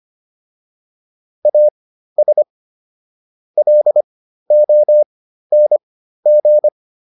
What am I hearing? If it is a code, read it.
Morse code, heard as a keyed tone, about 25 wpm, spaced as for 13 wpm, 600 Hz. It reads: AS LONG